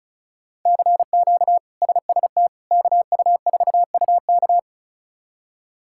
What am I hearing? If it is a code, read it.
Morse code, heard as a keyed tone, about 34 wpm, 700 Hz. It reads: CQ SST KU4UK
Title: K1USN SST QSO Practice - 35wpm